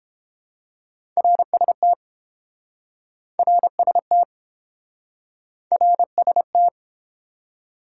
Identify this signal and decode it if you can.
Morse code, heard as a keyed tone, about 30 wpm, 700 Hz. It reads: RST RST RST